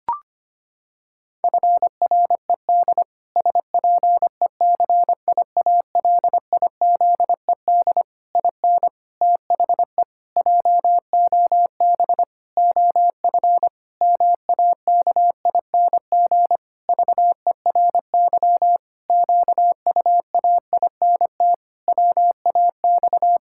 Morse code, heard as a keyed tone, about 25 wpm, 700 Hz. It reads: FRED SPECIALIZED IN THE JOB OF MAKING VERY QUAINT WAX
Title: Pangram Challenge v2 - 25wpm